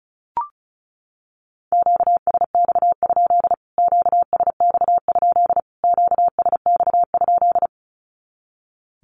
Morse code, heard as a keyed tone, about 34 wpm, 700 Hz. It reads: QSX? QSX? QSX?
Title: Q-Codes Repeated 3x 35wpm